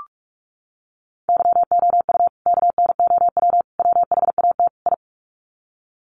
Morse code, heard as a keyed tone, about 44 wpm, 700 Hz. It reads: YOU KNOW WHAT I
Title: Sets of 4 Words 45wpm